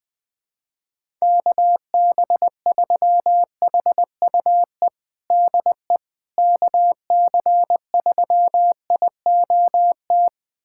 Morse code, heard as a keyed tone, about 21 wpm, 700 Hz. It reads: KB3HUE DE KC3IOT
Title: USA Callsign DE Callsign - 6 Characters - 20wpm